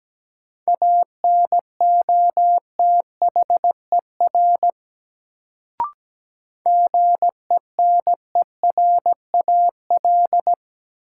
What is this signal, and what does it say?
Morse code, heard as a keyed tone, about 17 wpm, 700 Hz. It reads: ANOTHER GENERAL
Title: Sentences from Top 600 Words - New Word Drill - 17wpm